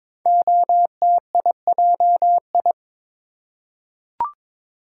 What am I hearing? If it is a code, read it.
Morse code, heard as a keyed tone, about 22 wpm, 700 Hz. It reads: OTIJI
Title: Code Groups - 5 Letters - 22wpm